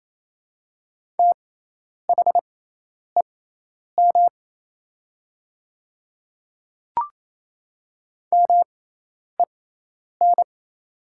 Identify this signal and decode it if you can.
Morse code, heard as a keyed tone, about 28 wpm, spaced as for 10 wpm, 700 Hz. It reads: THEM MEN
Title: 6x Character Spacing - Top 200 Words - 28wpm